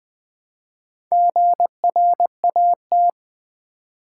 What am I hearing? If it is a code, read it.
Morse code, heard as a keyed tone, about 20 wpm, 700 Hz. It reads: GRAT
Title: Build A Word - 2 Letters At A Time - 8-Letter Words - 20wpm